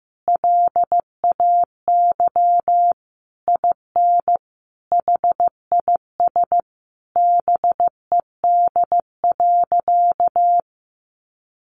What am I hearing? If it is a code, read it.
Morse code, heard as a keyed tone, about 14 wpm, 700 Hz. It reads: LAY IN HIS BED.
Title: Sentences from Top 500 Words 15wpm